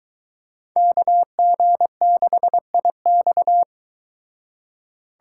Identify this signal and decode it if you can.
Morse code, heard as a keyed tone, about 22 wpm, 700 Hz. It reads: KG6IX